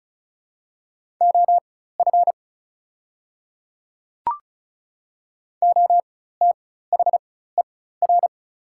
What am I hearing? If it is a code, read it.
Morse code, heard as a keyed tone, about 36 wpm, spaced as for 16 wpm, 700 Hz. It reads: OF OTHER
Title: 4x Character Spacing - Top 100 Words - 35wpm